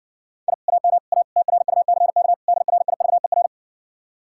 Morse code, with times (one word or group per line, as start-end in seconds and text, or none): 0.5-0.5 I
0.7-1.0 AM
1.1-1.2 A
1.4-2.3 TRUCK
2.5-3.5 DRIVER